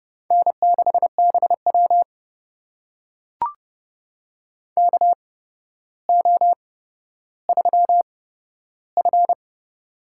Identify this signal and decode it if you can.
Morse code, heard as a keyed tone, about 30 wpm, spaced as for 13 wpm, 700 Hz. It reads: N6BW KO3F
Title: USA Call Signs - 4 Characters - 8x Character Spacing - 30wpm